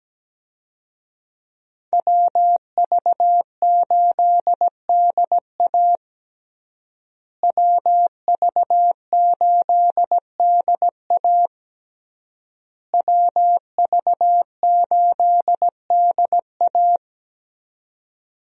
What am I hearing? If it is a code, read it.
Morse code, heard as a keyed tone, about 17 wpm, 700 Hz. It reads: WV8DA WV8DA WV8DA
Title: USA Call Signs - 5 Characters - Repeated 3x - 17wpm